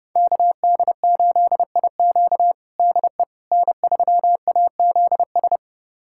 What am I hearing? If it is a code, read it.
Morse code, heard as a keyed tone, about 30 wpm, 700 Hz. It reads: KD8IQ DE N3AZS
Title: USA Callsign DE Callsign - 5 Characters - 30wpm